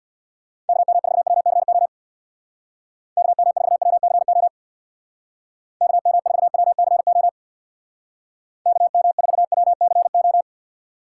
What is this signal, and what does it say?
Morse code, heard as a keyed tone, about 60 wpm, 700 Hz. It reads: KM4WCO KM4WCO KM4WCO KM4WCO
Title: Speed Racing - USA 2x3 Call Signs 50wpm